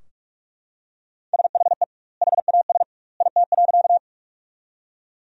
Morse code, heard as a keyed tone, about 42 wpm, 700 Hz. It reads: SHE HAS IT.
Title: 2x Word-Spacing - Sentences from Top 100 Words - 45wpm